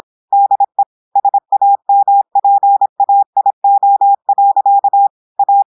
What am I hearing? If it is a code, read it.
Morse code, heard as a keyed tone, about 26 wpm, 800 Hz. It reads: DE SAMPAIO. A
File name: audios / portugais / portugais texte en clair 1 vitesse 26 mots minutes